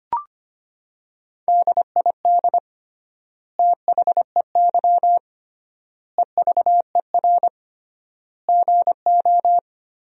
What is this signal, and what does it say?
Morse code, heard as a keyed tone, about 25 wpm, 700 Hz. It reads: DID THEY EVER GO
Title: 3x Word-Spacing - Sentences from Top 400 Words - 25wpm